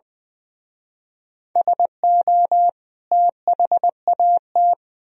Morse code, heard as a keyed tone, about 24 wpm, 700 Hz. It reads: SO THAT
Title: Speed-Racing - Sets of 2 Words 20wpm